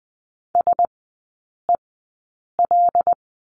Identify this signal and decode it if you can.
Morse code, heard as a keyed tone, about 20 wpm, spaced as for 8 wpm, 700 Hz. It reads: SEL
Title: 074 Lesson 13e - Letter L - Three Characters 20wpm